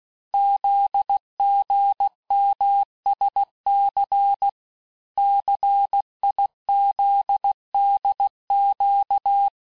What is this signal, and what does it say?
Morse code, heard as a keyed tone, about 16 wpm, 800 Hz. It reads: ZGMSC CIZDQ
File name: cours-du-lundi files / cours-1592--teste-chiffre-vitesse-15mpm